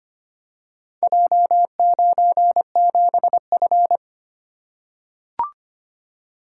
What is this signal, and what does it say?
Morse code, heard as a keyed tone, about 25 wpm, 700 Hz. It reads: J97F
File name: practice / 25wpm / Four Letter-Number Combinations - 25wpm